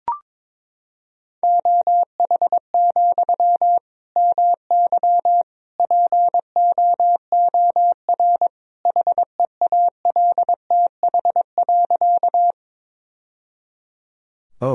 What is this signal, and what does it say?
Morse code, heard as a keyed tone, about 21 wpm, 700 Hz. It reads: OH, MY POOR HEALTH.
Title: Sentences from Top 1000 Words - Review Part 3 - 22wpm